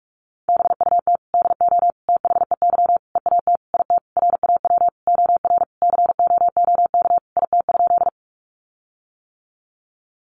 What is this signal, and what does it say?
Morse code, heard as a keyed tone, about 44 wpm, 700 Hz. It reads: BUT DO THEY EAT IT RAW OR COOK IT?
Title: Sentences from Top 1500 Words 45wpm